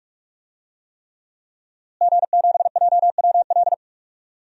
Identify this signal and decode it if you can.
Morse code, heard as a keyed tone, about 48 wpm, 700 Hz. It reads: G7JWL
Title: DXCC-100 Call Signs - 5 Characters - Repeated 3x - 45wpm